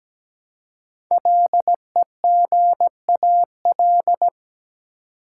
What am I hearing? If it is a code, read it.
Morse code, heard as a keyed tone, about 17 wpm, 700 Hz. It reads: LEGAL
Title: Five Letter Words - 17wpm